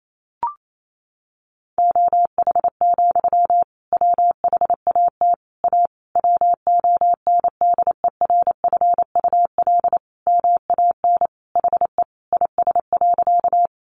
Morse code, heard as a keyed tone, about 28 wpm, 700 Hz. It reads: OH, WHAT A WONDERFUL MAN HE IS.
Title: Sentences from Top 1100 Words 28wpm